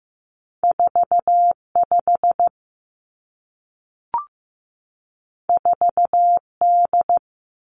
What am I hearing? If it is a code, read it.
Morse code, heard as a keyed tone, about 15 wpm, 700 Hz. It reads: 4 5 4 D
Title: Two Challenging Letter-Number Combinations 15wpm